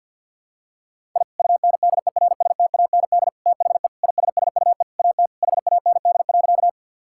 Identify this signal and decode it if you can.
Morse code, heard as a keyed tone, about 51 wpm, 700 Hz. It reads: I UNDERSTAND THE ISSUE AT HAND.
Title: Sentences from Top 1000 Words - Review Part 6 - 50wpm